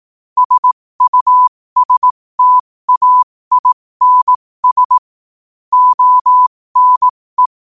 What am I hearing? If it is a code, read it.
Morse code, heard as a keyed tone, about 18 wpm, spaced as for 15 wpm, 1000 Hz. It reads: SUSTAINS ONE
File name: qotdfiles / 15wpm